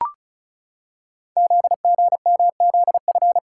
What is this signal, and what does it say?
Morse code, heard as a keyed tone, about 36 wpm, 690 Hz. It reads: ZGMZF
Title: Code Groups - 5 Letters - 35wpm